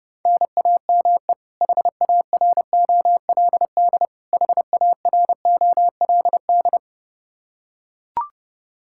Morse code, heard as a keyed tone, about 30 wpm, 700 Hz. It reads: NAME HAROLD HAROLD
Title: Practice QSO - Exchange 1 - Beginning - 30wpm